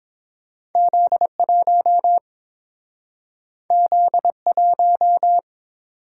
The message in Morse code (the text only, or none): Z1 Z1